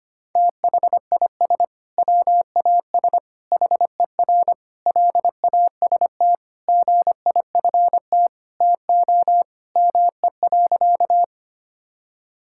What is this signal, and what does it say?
Morse code, heard as a keyed tone, about 25 wpm, 700 Hz. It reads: THIS WAS HER LAST GIFT TO ME.
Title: Sentences from Top 1000 Words - Review Part 6 - 25wpm